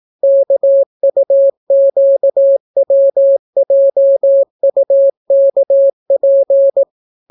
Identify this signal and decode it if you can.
Morse code, heard as a keyed tone, about 18 wpm, 550 Hz. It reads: KUQWJUKP